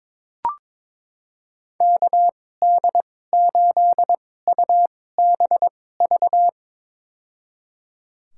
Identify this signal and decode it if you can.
Morse code, heard as a keyed tone, about 22 wpm, spaced as for 16 wpm, 700 Hz. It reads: KD8UBV
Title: USA Call Signs - 6 Characters - 2x Character Spacing - 22wpm